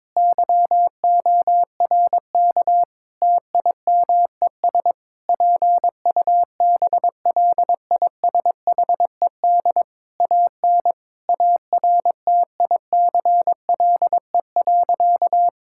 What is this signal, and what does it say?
Morse code, heard as a keyed tone, about 22 wpm, 700 Hz. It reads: YORK TIMES PUBLISHED AN ARTICLE.
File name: practice / 22wpm / Sentences with 4-Word Phrases - 22wpm